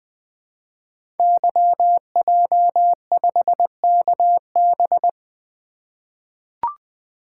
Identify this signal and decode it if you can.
Morse code, heard as a keyed tone, about 20 wpm, 700 Hz. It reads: YJ5KB
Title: Code Groups - 5 Letter:Numbers - 20wpm